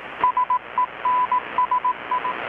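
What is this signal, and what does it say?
Morse code, heard as a keyed tone, about 18 wpm, 1000 Hz, with radio noise. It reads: SENSI
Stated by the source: Morse from KPH's Night of Nights XX maritime radiotelegraphy event, received in Chiapas, Mexico about 3,800 km away on a Kenwood R-600 with a grounded 10 m long-wire antenna.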